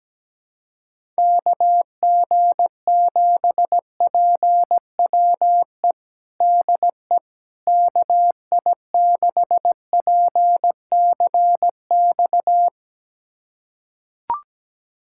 Morse code, heard as a keyed tone, about 17 wpm, 700 Hz. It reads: KG7PWE DE KI6PCX
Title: USA Callsign DE Callsign - 6 Characters - 17wpm